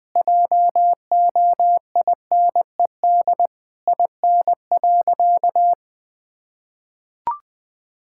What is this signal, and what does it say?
Morse code, heard as a keyed tone, about 20 wpm, 700 Hz. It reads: JOINED IN.
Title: Sentences with 2-Word Phrases - Spoken Beginning - 20wpm